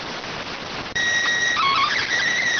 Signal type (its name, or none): SSTV (slow-scan television)